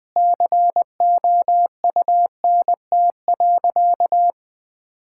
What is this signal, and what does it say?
Morse code, heard as a keyed tone, about 20 wpm, 700 Hz. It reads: COUNT.
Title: Difficult Sentences with Hints - 20wpm